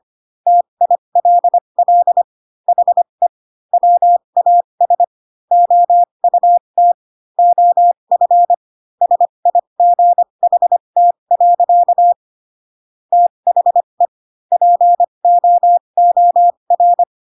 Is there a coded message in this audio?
Yes: Morse code from 0.5 to 17.0 s, heard as a keyed tone, about 25 wpm, spaced as for 22 wpm, 700 Hz. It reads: TILL HE WAS OUT OF SIGHT. THE POOR